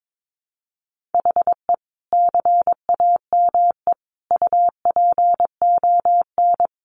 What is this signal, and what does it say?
Morse code, heard as a keyed tone, about 22 wpm, 700 Hz. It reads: HE CAME UPON
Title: Sentences from Top 1100 Words 22wpm